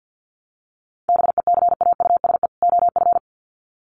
Morse code, heard as a keyed tone, about 48 wpm, 700 Hz. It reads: BECAUSE OF
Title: Sets of 2 Words 50wpm